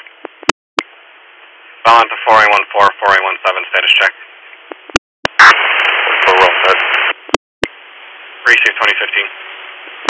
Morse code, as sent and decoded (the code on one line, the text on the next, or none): none